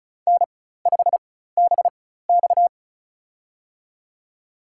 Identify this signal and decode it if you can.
Morse code, heard as a keyed tone, about 34 wpm, spaced as for 16 wpm, 700 Hz. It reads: N5BX
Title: USA Call Signs - 4x Character Spacing - 35wpm